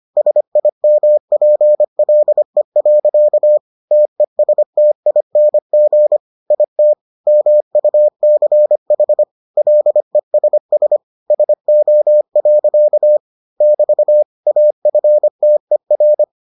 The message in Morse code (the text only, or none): SIMPLE. TESTING IT MUCH LESS SO. <BT> AFTER